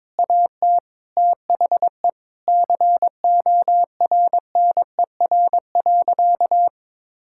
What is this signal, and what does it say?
Morse code, heard as a keyed tone, about 22 wpm, 700 Hz. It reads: AT THE CORNER.
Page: morse-code-ninja-practice-22wpm